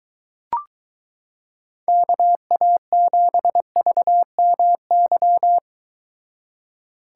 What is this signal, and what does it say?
Morse code, heard as a keyed tone, about 23 wpm, 700 Hz. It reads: KA7VMY